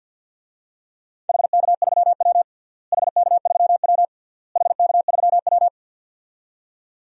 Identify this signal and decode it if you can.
Morse code, heard as a keyed tone, about 49 wpm, 700 Hz. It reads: HK3W HK3W HK3W
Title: DXCC-100 Call Signs - 4 Characters - Repeated 3x - 50wpm